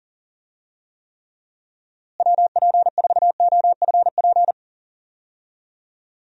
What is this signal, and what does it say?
Morse code, heard as a keyed tone, about 40 wpm, 700 Hz. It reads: WP4OFP
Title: USA Call Signs - 6 Characters - Repeated 2x - 40wpm